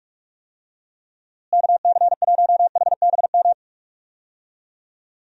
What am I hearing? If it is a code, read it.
Morse code, heard as a keyed tone, about 47 wpm, 700 Hz. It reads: KC1HBM
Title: USA Call Signs - 6 Characters - Repeated 3x - 45wpm